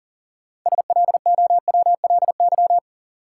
Morse code, heard as a keyed tone, about 40 wpm, 700 Hz. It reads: SLOWLY
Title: Sentences from Top 1100 Words - New Word Drill Alt - 40wpm